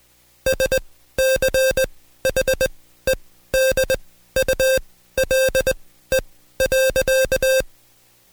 Morse code, heard as a keyed tone, about 20 wpm, spaced as for 13 wpm, 530 Hz. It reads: SCHEDULE.
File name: Code Quick / AudioDownld2 / Seg133-Stepper6 13WPM